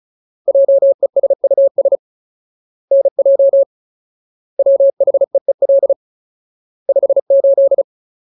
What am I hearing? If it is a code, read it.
Morse code, heard as a keyed tone, about 34 wpm, 550 Hz. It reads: JESUS NJ WHEEL 58
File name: Ultimate Word List / 35WPM Ultimate